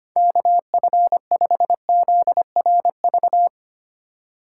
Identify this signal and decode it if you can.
Morse code, heard as a keyed tone, about 25 wpm, 700 Hz. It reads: KF5ZRV
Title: USA Call Signs - 6 Characters - 4x Character Spacing - 25wpm